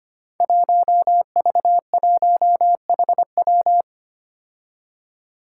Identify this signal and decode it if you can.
Morse code, heard as a keyed tone, about 25 wpm, 700 Hz. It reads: 1V1HW
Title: Five Challenging Letter-Number Combinations 25wpm